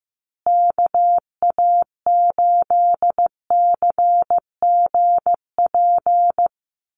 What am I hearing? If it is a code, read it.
Morse code, heard as a keyed tone, about 15 wpm, 700 Hz. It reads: KA8CGP
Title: USA Call Signs Repeated 2x 15wpm